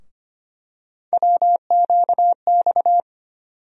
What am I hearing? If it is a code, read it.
Morse code, heard as a keyed tone, about 25 wpm, 700 Hz. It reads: WQX
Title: USA Call Sign Postfixes - 3 Letters 25wpm